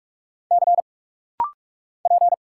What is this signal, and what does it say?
Morse code, heard as a keyed tone, about 44 wpm, 700 Hz. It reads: C P
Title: Single Letter-Number Alt - Rapid-Fire 45wpm